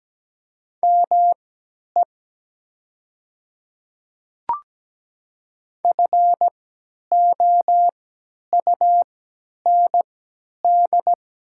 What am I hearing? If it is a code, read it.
Morse code, heard as a keyed tone, about 17 wpm, spaced as for 10 wpm, 700 Hz. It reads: ME FOUND
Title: 3x Character Spacing - Top 300 Words - 17wpm